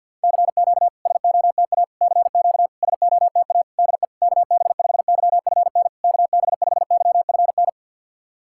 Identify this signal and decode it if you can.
Morse code, heard as a keyed tone, about 52 wpm, 690 Hz. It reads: CQ SOTA CQ SOTA DE KB5YFN KB5YFN
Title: SOTA QSO Practice 50wpm